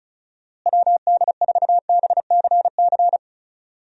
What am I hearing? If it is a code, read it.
Morse code, heard as a keyed tone, about 36 wpm, 700 Hz. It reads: WD4BCC
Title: USA Call Signs - 6 Characters - Repeated 3x - 35wpm